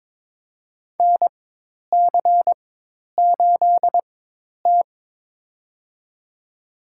Morse code, heard as a keyed tone, about 23 wpm, spaced as for 10 wpm, 700 Hz. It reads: NC8T
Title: USA Call Signs - 4 Characters - 4x Character Spacing - 22wpm